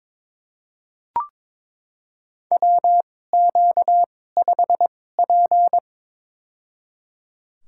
Morse code, heard as a keyed tone, about 22 wpm, spaced as for 16 wpm, 700 Hz. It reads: WQ5P
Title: USA Call Signs - 4 Characters - 2x Character Spacing - 22wpm